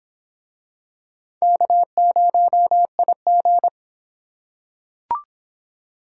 Morse code, heard as a keyed tone, about 24 wpm, 700 Hz. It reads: K0IG